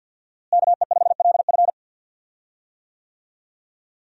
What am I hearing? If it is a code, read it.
Morse code, heard as a keyed tone, about 52 wpm, 700 Hz. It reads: KE5LF
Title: USA Call Signs - 5 Characters - Repeated 2x - 50wpm